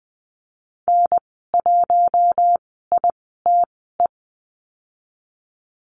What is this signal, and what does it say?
Morse code, heard as a keyed tone, about 20 wpm, spaced as for 15 wpm, 700 Hz. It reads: N1ITE
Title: 082 Lesson 14g - Number 2 - Call Signs 20wpm